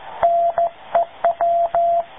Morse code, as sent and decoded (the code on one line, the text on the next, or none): -. .--
NEW